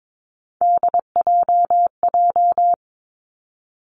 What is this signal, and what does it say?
Morse code, heard as a keyed tone, about 22 wpm, 700 Hz. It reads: DJJ
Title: Three Challenging Letter-Number Combinations 22wpm